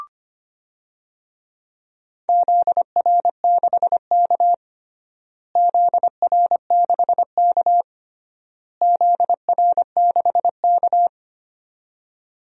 Morse code, heard as a keyed tone, about 25 wpm, 700 Hz. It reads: ZR6K ZR6K ZR6K
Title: DXCC-100 Call Signs - 4 Characters - Repeated 3x - 25wpm